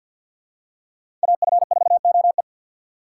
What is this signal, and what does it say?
Morse code, heard as a keyed tone, about 47 wpm, 700 Hz. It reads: AF4OE
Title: USA Call Signs - 5 Characters - 4x Character Spacing - 50wpm